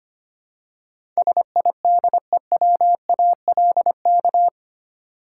Morse code, heard as a keyed tone, about 25 wpm, 700 Hz. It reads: SIDEWALK
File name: 8 Letter Words - First 3 Letters Given - 25wpm